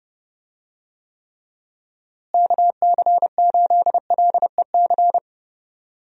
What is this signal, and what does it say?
Morse code, heard as a keyed tone, about 30 wpm, 700 Hz. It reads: KC8LEC
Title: USA Call Signs - 6 Characters - Repeated 3x - 30wpm